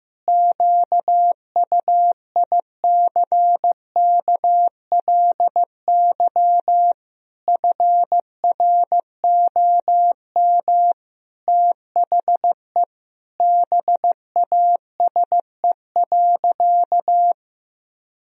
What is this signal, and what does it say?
Morse code, heard as a keyed tone, about 16 wpm, 700 Hz. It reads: QUICKLY FROM THE BASE.
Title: Sentences from Top 1000 Words - Review Part 1 - 15wpm